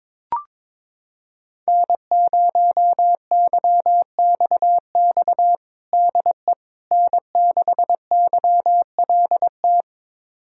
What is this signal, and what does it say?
Morse code, heard as a keyed tone, about 22 wpm, 700 Hz. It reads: N0YXX DE N6YLT
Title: USA Callsign DE Callsign - 5 Characters - 22wpm